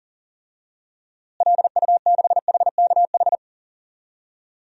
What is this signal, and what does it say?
Morse code, heard as a keyed tone, about 40 wpm, 700 Hz. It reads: LU6HKH
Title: DXCC-100 Call Signs - 6 Characters - Repeated 3x - 40wpm